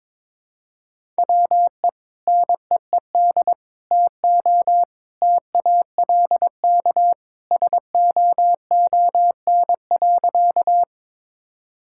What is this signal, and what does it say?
Morse code, heard as a keyed tone, about 22 wpm, 700 Hz. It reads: WE NEED TO TALK SOON.
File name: practice / 22wpm / Sentences with 2-Word Phrases - 22wpm